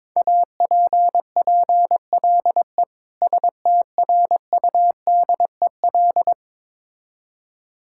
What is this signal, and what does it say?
Morse code, heard as a keyed tone, about 22 wpm, 700 Hz. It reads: APPLE STRUDEL
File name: Difficult Words with Hints - 22wpm